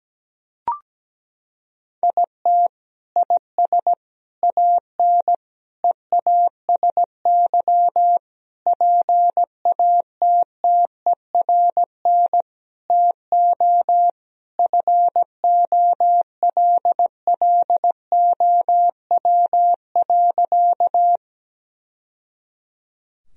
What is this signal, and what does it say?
Morse code, heard as a keyed tone, about 17 wpm, 700 Hz. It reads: IT IS AN EASY PATTERN TO FOLLOW.
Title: Sentences from Top 1000 Words - Review Part 2 - 17wpm